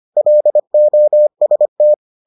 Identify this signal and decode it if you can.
Morse code, heard as a keyed tone, about 25 wpm, 600 Hz. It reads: LOST